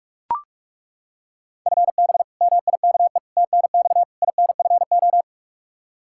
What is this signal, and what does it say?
Morse code, heard as a keyed tone, about 41 wpm, 700 Hz. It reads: FB MIKE TNX INFO
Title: Practice QSO - Exchange 3 - More Info - 45wpm